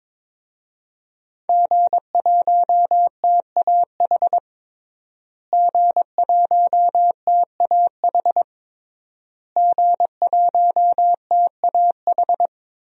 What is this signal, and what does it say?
Morse code, heard as a keyed tone, about 22 wpm, 700 Hz. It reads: G1TAH G1TAH G1TAH
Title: DXCC-100 Call Signs - 5 Characters - Repeated 3x - 22wpm